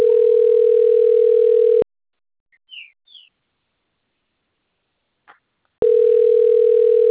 Ringback tone (North America), 440 + 480 Hz.